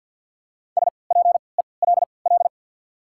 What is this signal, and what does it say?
Morse code, heard as a keyed tone, about 53 wpm, spaced as for 29 wpm, 700 Hz. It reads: SPELL
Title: 3x Character Spacing - Top 200 Words - 50wpm